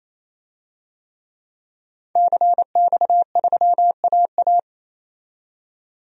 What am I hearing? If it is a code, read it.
Morse code, heard as a keyed tone, about 28 wpm, 700 Hz. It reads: CX3AA
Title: DXCC-100 Call Signs - 5 Characters - 28wpm